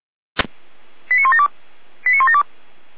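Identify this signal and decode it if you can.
Five-tone selective call (ZVEI), heard as quick runs of five tones, tones of about 70 ms. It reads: 89172 89172
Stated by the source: Five-tone ZVEI selective call 89582 received by SDR, archived by Feuerwehr Orlamünde fire brigade.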